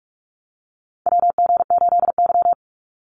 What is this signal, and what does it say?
Morse code, heard as a keyed tone, about 46 wpm, 700 Hz. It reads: WG8Y
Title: NA SOTA Chaser Callsigns 45wpm